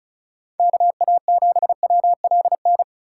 Morse code, heard as a keyed tone, about 36 wpm, 700 Hz. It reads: KA7WLN
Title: USA Call Signs - 6x Character Spacing - 35wpm